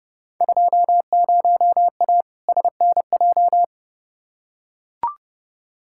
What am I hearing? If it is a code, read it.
Morse code, heard as a keyed tone, about 30 wpm, 700 Hz. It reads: 20A SNJ
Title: Field Day Exchanges Encore - 30wpm